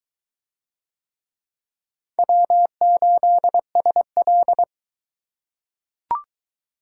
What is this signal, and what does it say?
Morse code, heard as a keyed tone, about 21 wpm, 700 Hz. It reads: W8SL